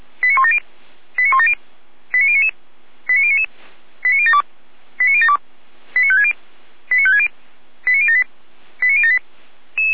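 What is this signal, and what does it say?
Five-tone selective call (ZVEI), heard as quick runs of five tones, tones of about 70 ms. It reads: 89170 89170 89099 89099 89072 89072 89560 89560 89078 89078
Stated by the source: ZVEI five-tone call 89169 received by SDR.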